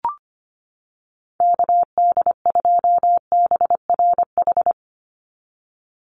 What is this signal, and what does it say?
Morse code, heard as a keyed tone, about 25 wpm, 700 Hz. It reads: KD2BRH
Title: USA Call Signs 25wpm